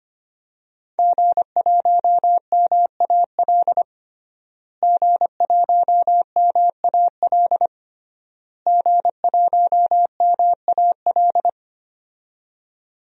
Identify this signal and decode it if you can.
Morse code, heard as a keyed tone, about 25 wpm, 700 Hz. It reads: G1MAL G1MAL G1MAL
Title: DXCC-100 Call Signs - 5 Characters - Repeated 3x - 25wpm